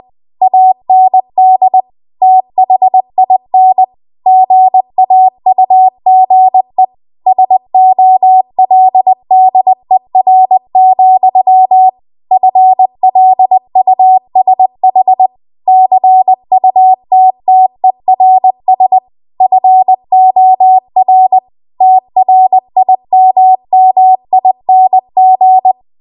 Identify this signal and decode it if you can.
Morse code, heard as a keyed tone, about 20 wpm, 750 Hz. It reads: AND THIN GAUGE SOLDER, FLUSH CUTTERS FOR TRIMMING